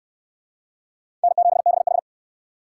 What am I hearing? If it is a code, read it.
Morse code, heard as a keyed tone, about 69 wpm, 700 Hz. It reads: N7BH